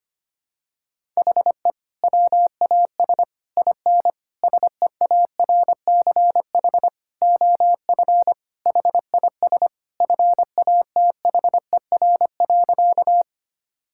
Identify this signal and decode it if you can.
Morse code, heard as a keyed tone, about 25 wpm, 700 Hz. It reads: HE WAS IN SEARCH OF HIS FATHER.
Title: Sentences from Top 1000 Words - Review Part 5 - 25wpm